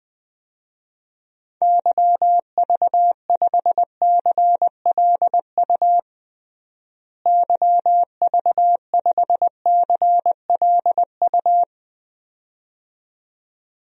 Morse code, heard as a keyed tone, about 20 wpm, 700 Hz. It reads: YV5CLU YV5CLU
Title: DXCC-100 Call Signs - 6 Characters - Repeated 2x - 20wpm